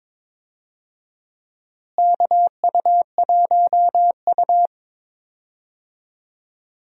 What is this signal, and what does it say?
Morse code, heard as a keyed tone, about 22 wpm, 700 Hz. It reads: KU1U
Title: USA Call Signs - 4 Characters - Repeated 2x - 22wpm